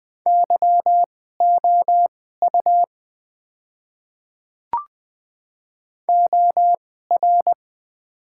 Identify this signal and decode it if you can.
Morse code, heard as a keyed tone, about 20 wpm, spaced as for 15 wpm, 700 Hz. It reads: YOU OR
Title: 2x Character Spacing - Top 100 Words - 20wpm